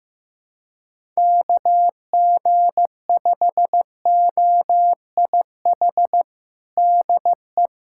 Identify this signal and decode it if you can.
Morse code, heard as a keyed tone, about 15 wpm, 700 Hz. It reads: KG5OIH DE
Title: USA Callsign DE Callsign - 6 Characters - 15wpm